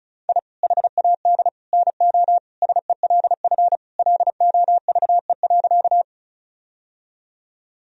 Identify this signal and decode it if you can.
Morse code, heard as a keyed tone, about 36 wpm, 700 Hz. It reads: I HAD NO SELF LOVE.
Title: Sentences from Top 1000 Words - Review Part 1 - 35wpm